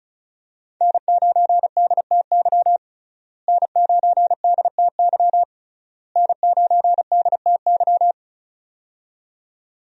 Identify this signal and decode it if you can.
Morse code, heard as a keyed tone, about 36 wpm, 700 Hz. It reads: N9DTY N9DTY N9DTY
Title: USA Call Signs - 5 Characters - Repeated 3x - 35wpm